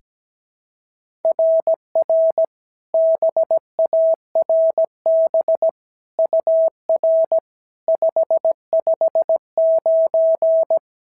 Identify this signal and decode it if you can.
Morse code, heard as a keyed tone, about 16 wpm, 650 Hz. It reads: RR BARB UR 559
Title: SOTA QSO Practice 17wpm